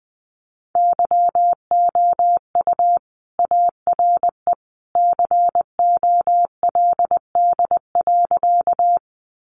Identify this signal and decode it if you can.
Morse code, heard as a keyed tone, about 20 wpm, 700 Hz. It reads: YOU ARE COLD.
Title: Sentences from Top 400 Words 20wpm